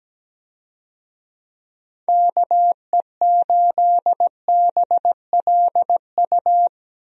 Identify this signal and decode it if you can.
Morse code, heard as a keyed tone, about 17 wpm, 700 Hz. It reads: KE8BLU
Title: USA Call Signs - 6 Characters - Repeated 3x - 17wpm